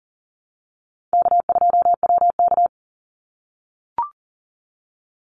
Morse code, heard as a keyed tone, about 40 wpm, 700 Hz. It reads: K2WK
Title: Easy Call Signs - 40wpm